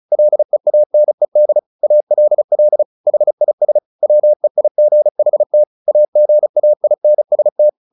Morse code, heard as a keyed tone, about 34 wpm, 600 Hz. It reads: LEANED ALL HIS WEIGHT AGAINST
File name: the old man and the sea 2 35wpm